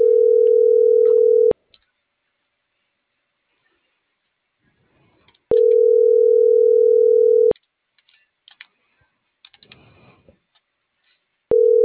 Ringback tone (North America), 440 + 480 Hz.